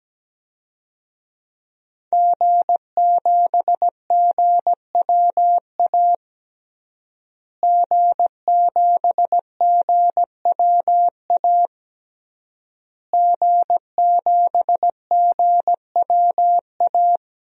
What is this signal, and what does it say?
Morse code, heard as a keyed tone, about 17 wpm, 700 Hz. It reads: G7GWA G7GWA G7GWA
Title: DXCC-100 Call Signs - 5 Characters - Repeated 3x - 17wpm